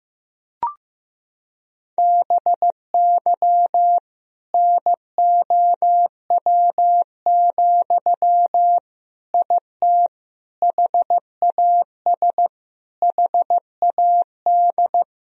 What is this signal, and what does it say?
Morse code, heard as a keyed tone, about 15 wpm, 700 Hz. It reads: BY NOW, IT HAS HAD
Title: Sentences from Top 1000 Words - Review Part 1 - 15wpm